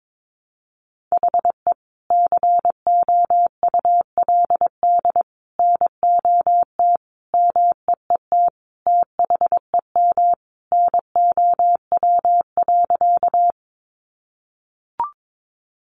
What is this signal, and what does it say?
Morse code, heard as a keyed tone, about 22 wpm, 700 Hz. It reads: HE COULD NOT MEET THEM NOW.
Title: Sentences from Top 1100 Words 22wpm